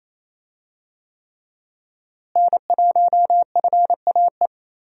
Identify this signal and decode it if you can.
Morse code, heard as a keyed tone, about 28 wpm, 700 Hz. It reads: N1FAE
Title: USA Call Signs - 5 Characters - Repeated 3x - 28wpm